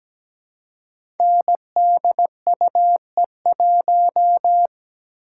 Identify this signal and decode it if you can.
Morse code, heard as a keyed tone, about 17 wpm, 700 Hz. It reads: NDUE1
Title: Code Groups - 5 Letter:Numbers - 17wpm